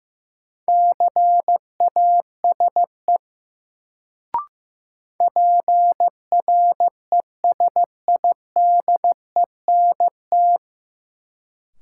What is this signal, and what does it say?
Morse code, heard as a keyed tone, about 15 wpm, 700 Hz. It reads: CASE PRESIDENT
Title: Sets of 2 Words - New Word Drill - 15wpm